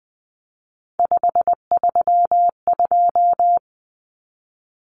Morse code, heard as a keyed tone, about 20 wpm, 700 Hz. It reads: 532
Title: Common Cut Numbers - 3-Digit 20wpm